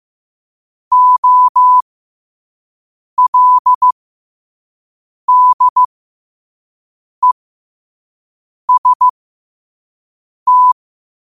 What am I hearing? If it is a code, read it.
Morse code, heard as a keyed tone, about 15 wpm, spaced as for 5 wpm, 1000 Hz. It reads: OLDEST
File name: qotdfiles / 5wpm